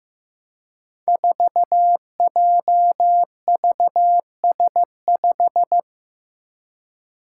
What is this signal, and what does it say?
Morse code, heard as a keyed tone, about 15 wpm, 700 Hz. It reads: 4JVS5
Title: Five Challenging Letter-Number Combinations 15wpm